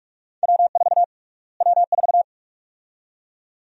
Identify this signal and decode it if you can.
Morse code, heard as a keyed tone, about 46 wpm, 700 Hz. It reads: W4 W4
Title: USA Call Sign Prefixes - Repeated 2x - 45wpm